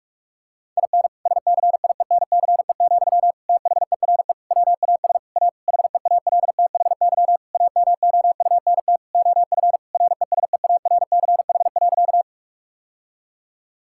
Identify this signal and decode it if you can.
Morse code, heard as a keyed tone, about 46 wpm, 700 Hz. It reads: IN SCIENCE, THERE WAS A HEALTHY AMOUNT OF RESEARCH.
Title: Sentences from Top 1000 Words - Review Part 5 - 45wpm